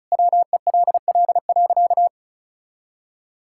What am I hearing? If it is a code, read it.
Morse code, heard as a keyed tone, about 36 wpm, 700 Hz. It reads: WELL.
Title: Sentences from Top 1000 Words - Review Part 6 - 35wpm